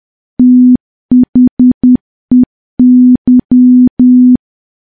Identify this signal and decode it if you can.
Morse code, heard as a keyed tone, about 10 wpm, 250 Hz. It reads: THEY
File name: morse / morse-training-10WPM